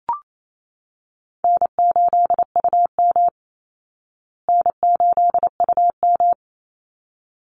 Morse code, heard as a keyed tone, about 28 wpm, 700 Hz. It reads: N8UM N8UM
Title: Call Signs Repeated 3x - 28wpm